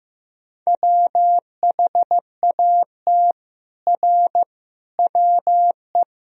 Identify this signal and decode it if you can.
Morse code, heard as a keyed tone, about 16 wpm, 700 Hz. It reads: WHAT R WE
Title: Sentences from Top 100 Words and Abbreviations 15wpm